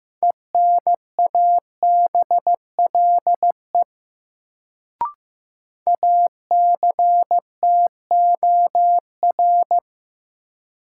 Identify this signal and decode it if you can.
Morse code, heard as a keyed tone, about 15 wpm, 700 Hz. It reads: ENABLE ACTOR
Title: Sentences from Top 1500 Words - New Word Drill Alt - 15wpm